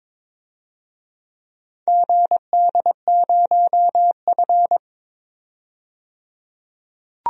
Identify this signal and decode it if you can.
Morse code, heard as a keyed tone, about 22 wpm, 700 Hz. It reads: GD0F
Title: DXCC-100 Call Signs - 4 Characters - 22wpm